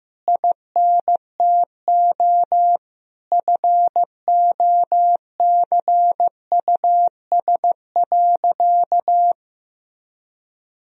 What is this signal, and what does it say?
Morse code, heard as a keyed tone, about 15 wpm, 700 Hz. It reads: INTO FOCUS.
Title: Sentences from Top 1000 Words - Review Part 4 - 15wpm